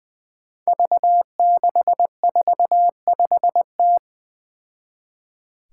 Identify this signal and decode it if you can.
Morse code, heard as a keyed tone, about 20 wpm, 700 Hz. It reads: V645T
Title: Five Challenging Letter-Number Combinations 20wpm